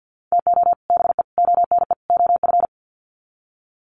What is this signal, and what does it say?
Morse code, heard as a keyed tone, about 50 wpm, 700 Hz. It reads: TO BE ONE OF
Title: Sets of 4 Words 50wpm